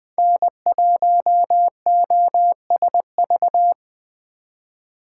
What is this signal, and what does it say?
Morse code, heard as a keyed tone, about 20 wpm, 700 Hz. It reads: N1OSV
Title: USA Call Signs - 4x Character Spacing - 20wpm